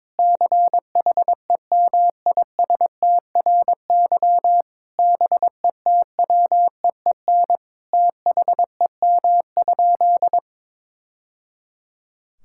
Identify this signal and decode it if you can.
Morse code, heard as a keyed tone, about 22 wpm, 700 Hz. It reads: CHEMISTRY BETWEEN THEM?
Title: Sentences from Top 1000 Words - Review Part 6 - 22wpm